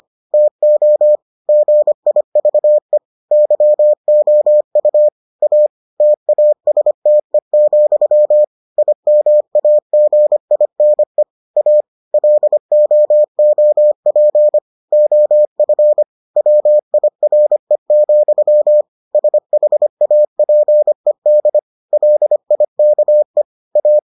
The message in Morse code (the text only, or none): TO GIVE YOU A TASTE, IMAGINE A LOOP OF WIRE, SHAPED LIKE A